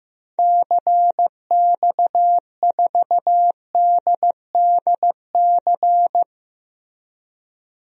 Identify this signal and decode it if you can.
Morse code, heard as a keyed tone, about 15 wpm, 700 Hz. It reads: CX4DDC
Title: DXCC-100 Call Signs - 6 Characters - Repeated 3x - 15wpm